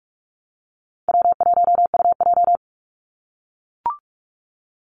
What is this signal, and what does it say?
Morse code, heard as a keyed tone, about 44 wpm, 700 Hz. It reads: W1UJ
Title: Call Signs - 45wpm